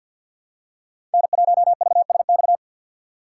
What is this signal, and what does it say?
Morse code, heard as a keyed tone, about 53 wpm, 700 Hz. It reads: N1VSX
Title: USA Call Signs - 3x Character Spacing - 50wpm